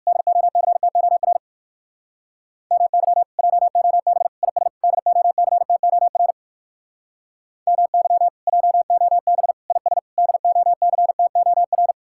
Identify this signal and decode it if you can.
Morse code, heard as a keyed tone, about 59 wpm, 700 Hz. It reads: DOCTOR MY JOB IS DOCTOR MY JOB IS DOCTOR